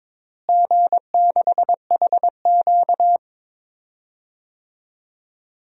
Morse code, heard as a keyed tone, about 22 wpm, 700 Hz. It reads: G6HQ
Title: DXCC-100 Call Signs - 4 Characters - Repeated 3x - 22wpm